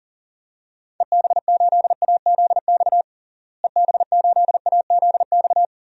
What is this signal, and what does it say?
Morse code, heard as a keyed tone, about 37 wpm, 700 Hz. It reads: EB8AZX EB8AZX
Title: DXCC-100 Call Signs - 6 Characters - Repeated 2x - 40wpm